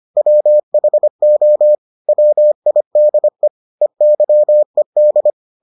Morse code, heard as a keyed tone, about 25 wpm, 600 Hz. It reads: WHO WIDE EYED